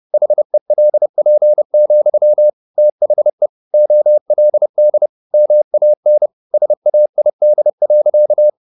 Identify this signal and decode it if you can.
Morse code, heard as a keyed tone, about 30 wpm, 600 Hz. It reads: HELP, THE OLD MAN SAID.
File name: the old man and the sea / the old man and the sea 1 30wpm